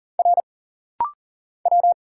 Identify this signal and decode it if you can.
Morse code, heard as a keyed tone, about 40 wpm, 700 Hz. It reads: R W
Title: Single Letters - Rapid-Fire 40wpm